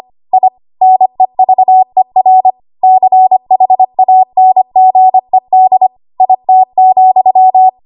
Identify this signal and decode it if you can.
Morse code, heard as a keyed tone, about 25 wpm, 750 Hz. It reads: I NEVER CHANGED IT,
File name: Archive / 25 WPM / 25WPM